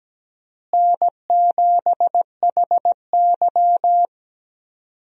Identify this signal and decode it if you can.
Morse code, heard as a keyed tone, about 17 wpm, 700 Hz. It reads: N7HY
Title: USA Call Signs - 4 Characters - Repeated 2x - 17wpm